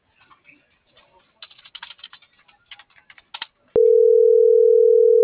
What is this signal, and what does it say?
Ringback tone (North America), 440 + 480 Hz.